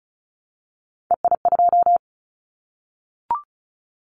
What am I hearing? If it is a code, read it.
Morse code, heard as a keyed tone, about 34 wpm, 700 Hz. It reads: EI2
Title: Callsign Prefixes for Top 50 DXCC - 35wpm